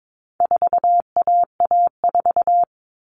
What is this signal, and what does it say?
Morse code, heard as a keyed tone, about 22 wpm, 700 Hz. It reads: 4AA4
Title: All Cut Numbers - 4-Digit 22wpm